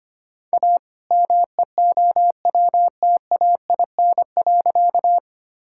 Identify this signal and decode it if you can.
Morse code, heard as a keyed tone, about 25 wpm, 700 Hz. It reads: A MEOWTAIN.
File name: practice / 25wpm / All-Age Appropriate Jokes - 25wpm